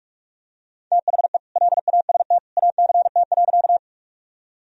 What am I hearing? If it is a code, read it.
Morse code, heard as a keyed tone, about 41 wpm, 700 Hz. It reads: THE LAST ACT.
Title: Sentences from Top 1000 Words - Review Part 1 - 45wpm